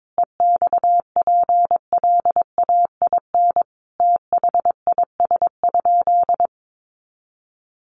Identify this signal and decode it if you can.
Morse code, heard as a keyed tone, about 22 wpm, 700 Hz. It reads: EXPLAIN THIS?
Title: Sentences from Top 1100 Words 22wpm